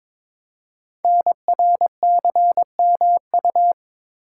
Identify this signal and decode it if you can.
Morse code, heard as a keyed tone, about 22 wpm, 700 Hz. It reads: NRCMU
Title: Code Groups - 5 Letters - 22wpm